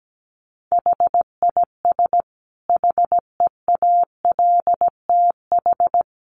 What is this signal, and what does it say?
Morse code, heard as a keyed tone, about 17 wpm, 700 Hz. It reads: HIS HEALTH
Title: Sentences from Top 1400 Words 17wpm